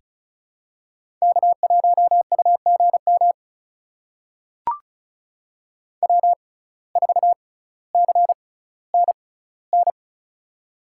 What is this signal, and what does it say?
Morse code, heard as a keyed tone, about 34 wpm, spaced as for 18 wpm, 700 Hz. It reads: K1UGM W4CNN
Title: USA Call Signs - 5 Characters - 6x Character Spacing - 35wpm